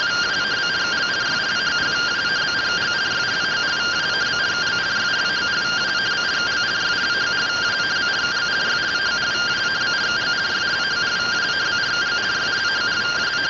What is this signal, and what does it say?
Two-tone teleprinter signal, about 170 Hz shift.